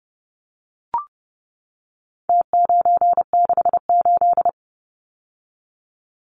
Morse code, heard as a keyed tone, about 29 wpm, 700 Hz. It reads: T968
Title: Numbers - Contest Style - 4-Digit 30wpm